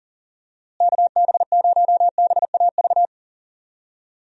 Morse code, heard as a keyed tone, about 38 wpm, 700 Hz. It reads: KB0BAV
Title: USA Call Signs - 6 Characters - Repeated 2x - 40wpm